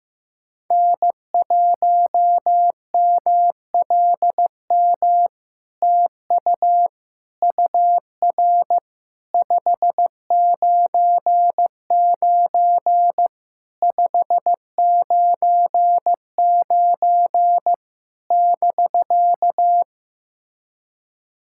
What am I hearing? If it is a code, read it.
Morse code, heard as a keyed tone, about 15 wpm, 700 Hz. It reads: N1MLM TU UR 599 599 <BK>
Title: SOTA QSO Practice 15wpm